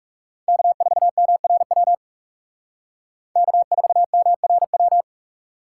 Morse code, heard as a keyed tone, about 38 wpm, 700 Hz. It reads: K4MRW K4MRW